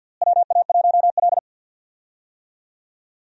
Morse code, heard as a keyed tone, about 49 wpm, 700 Hz. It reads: WA1L
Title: USA Call Signs - 4 Characters - Repeated 2x - 50wpm